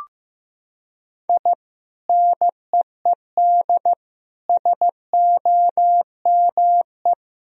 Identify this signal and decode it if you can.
Morse code, heard as a keyed tone, about 15 wpm, 700 Hz. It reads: I NEED SOME
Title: Sentences from Top 1000 Words - Review Part 1 - 15wpm